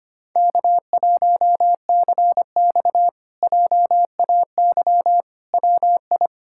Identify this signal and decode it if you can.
Morse code, heard as a keyed tone, about 25 wpm, 700 Hz. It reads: K1CX JAY WI
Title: K1USN SST QSO Practice - 25wpm